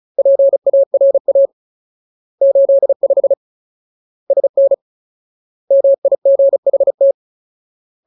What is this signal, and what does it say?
Morse code, heard as a keyed tone, about 34 wpm, 550 Hz. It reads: PARA 85 SN MIGHT
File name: Ultimate Word List / 35WPM Ultimate